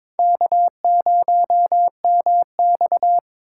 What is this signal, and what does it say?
Morse code, heard as a keyed tone, about 22 wpm, 700 Hz. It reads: K0MX
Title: USA Call Signs - 4x Character Spacing - 22wpm